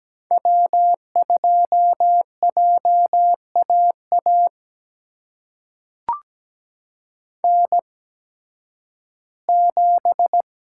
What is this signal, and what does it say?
Morse code, heard as a keyed tone, about 17 wpm, spaced as for 11 wpm, 700 Hz. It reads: W2JAA N7
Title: USA Call Signs - 8x Character Spacing - 17wpm